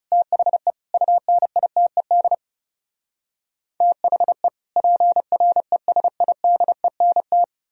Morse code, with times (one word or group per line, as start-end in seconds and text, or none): 0.1-0.7 THE
0.9-2.3 UNITED
3.8-4.5 THE
4.8-7.4 PRESIDENT